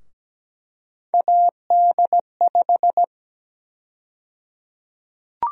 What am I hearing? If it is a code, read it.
Morse code, heard as a keyed tone, about 17 wpm, 700 Hz. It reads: AD5
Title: USA Call Sign Prefixes - Repeated 2x - 17wpm